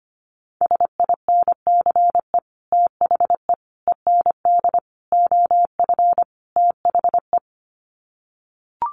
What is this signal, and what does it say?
Morse code, heard as a keyed tone, about 25 wpm, 700 Hz. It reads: SINCE THE END OF THE
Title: Sets of 5 Words Encore 25wpm